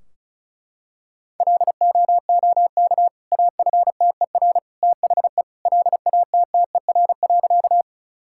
Morse code, heard as a keyed tone, about 34 wpm, 700 Hz. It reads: LOOK AFTER THE LATTER.
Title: Sentences from Top 1600 Words 35wpm